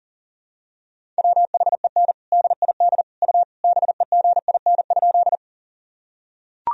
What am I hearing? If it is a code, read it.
Morse code, heard as a keyed tone, about 40 wpm, 700 Hz. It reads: WHEN DID U BEGIN?
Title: Sentences from Top 300 Words and Abbreviations 40wpm